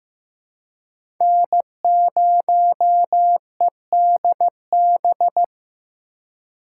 Morse code, heard as a keyed tone, about 16 wpm, 700 Hz. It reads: N0EDB
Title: USA Call Signs - 5 Characters - 4x Character Spacing - 15wpm